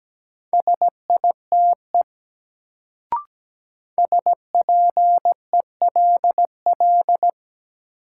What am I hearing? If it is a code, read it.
Morse code, heard as a keyed tone, about 17 wpm, 700 Hz. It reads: SITE SPELL
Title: Sentences from Top 1000 Words - Review Part 1 - New Word Drill - 17wpm